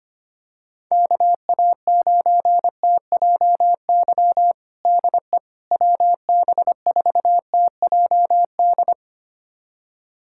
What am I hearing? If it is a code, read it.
Morse code, heard as a keyed tone, about 25 wpm, 700 Hz. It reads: KA9TJY DE WB4TJD
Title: USA Callsign DE Callsign 25wpm